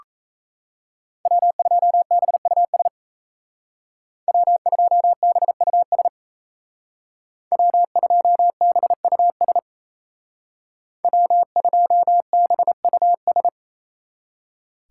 Morse code, heard as a keyed tone, about 36 wpm, 700 Hz. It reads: W2BUS W2BUS W2BUS W2BUS